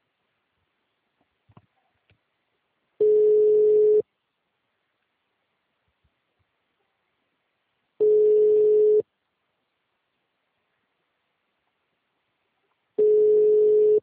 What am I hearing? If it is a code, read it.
Ringback tone (Europe, 425 Hz).